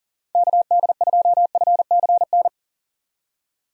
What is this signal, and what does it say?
Morse code, heard as a keyed tone, about 40 wpm, 700 Hz. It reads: KD2FCN
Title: USA Call Signs - 6 Characters - 40wpm